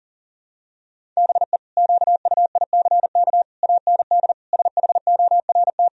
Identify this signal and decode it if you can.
Morse code, heard as a keyed tone, about 40 wpm, 700 Hz. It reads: BE QUICK AND SHORT
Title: Sentences from Top 1000 Words - Review Part 2 - 40wpm